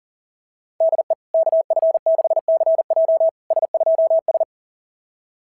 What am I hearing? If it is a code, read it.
Morse code, heard as a keyed tone, about 37 wpm, 660 Hz. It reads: DE KF6CJ S2S
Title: SOTA QSO Practice 40wpm